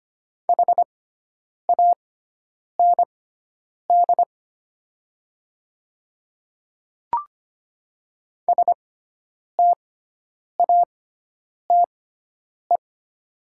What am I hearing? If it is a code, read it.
Morse code, heard as a keyed tone, about 25 wpm, spaced as for 9 wpm, 700 Hz. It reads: HAND STATE
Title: 6x Character Spacing - Top 500 Words - 25wpm